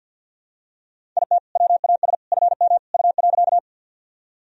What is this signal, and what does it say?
Morse code, heard as a keyed tone, about 51 wpm, 700 Hz. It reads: IT WAS FM U.
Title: Sentences from Top 100 Words and Abbreviations 50wpm